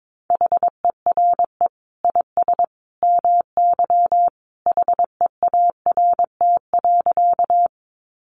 Morse code, heard as a keyed tone, about 22 wpm, 700 Hz. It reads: HERE IS MY HEART.
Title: Sentences from Top 1200 Words 22wpm